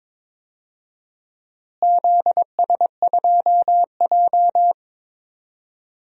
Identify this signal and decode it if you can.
Morse code, heard as a keyed tone, about 22 wpm, 700 Hz. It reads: ZS2J
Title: DXCC-100 Call Signs - 4 Characters - Repeated 3x - 22wpm